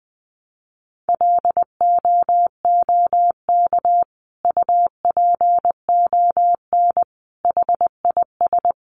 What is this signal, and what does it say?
Morse code, heard as a keyed tone, about 20 wpm, 700 Hz. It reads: LOOK UPON HIS
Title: Sentences from Top 1100 Words 20wpm